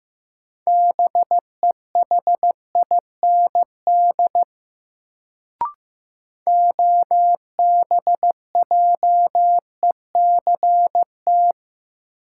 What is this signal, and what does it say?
Morse code, heard as a keyed tone, about 16 wpm, 700 Hz. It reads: BEHIND OBJECT
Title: Sentences from Top 500 Words - New Word Drill - 15wpm